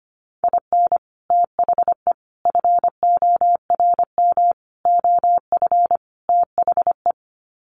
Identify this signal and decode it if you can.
Morse code, heard as a keyed tone, about 25 wpm, 700 Hz. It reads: IN THE FORM OF THE
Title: Sets of 5 Words Encore2 25wpm